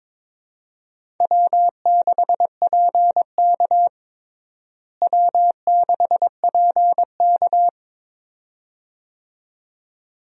Morse code, heard as a keyed tone, about 23 wpm, 700 Hz. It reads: W6PK W6PK
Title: USA Call Signs - 4 Characters - Repeated 2x - 22wpm